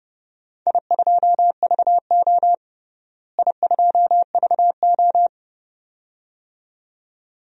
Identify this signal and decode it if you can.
Morse code, heard as a keyed tone, about 30 wpm, 700 Hz. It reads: I2VO I2VO
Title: DXCC-100 Call Signs - 4 Characters - Repeated 2x - 30wpm